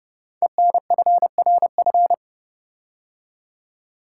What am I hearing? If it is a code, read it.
Morse code, heard as a keyed tone, about 28 wpm, 700 Hz. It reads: ENFRF
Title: Code Groups - 5 Letters - 30wpm